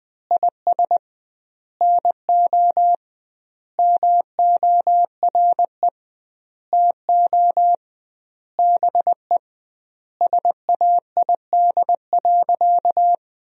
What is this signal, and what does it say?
Morse code, heard as a keyed tone, about 20 wpm, 700 Hz. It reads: IS NO MORE TO BE SAID.
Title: 2x Word-Spacing - Sentences from Top 100 Words - 20wpm